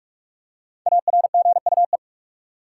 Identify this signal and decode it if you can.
Morse code, heard as a keyed tone, about 41 wpm, 700 Hz. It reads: ARGUE
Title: Five Letter Words - 45wpm